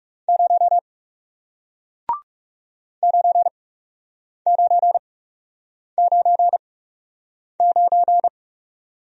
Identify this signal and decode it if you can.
Morse code, heard as a keyed tone, about 39 wpm, spaced as for 8 wpm, 700 Hz. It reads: 0 9999